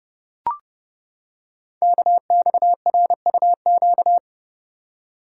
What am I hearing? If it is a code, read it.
Morse code, heard as a keyed tone, about 30 wpm, 700 Hz. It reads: KXRUQ
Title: Code Groups - 5 Letters - 30wpm